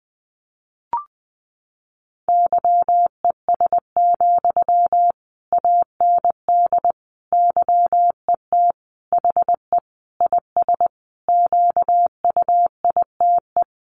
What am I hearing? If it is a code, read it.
Morse code, heard as a keyed tone, about 20 wpm, 700 Hz. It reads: YES, AND YET HE IS QUITE
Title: Sentences from Top 700 Words 20wpm